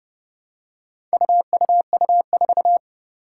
Morse code, heard as a keyed tone, about 30 wpm, 700 Hz. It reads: UUU4
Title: Four Challenging Letter-Number Combinations 30wpm